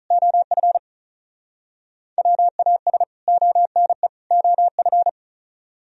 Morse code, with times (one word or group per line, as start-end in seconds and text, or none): 0.1-0.8 OF
2.2-3.0 WAS
3.3-4.1 ONE
4.3-5.1 OF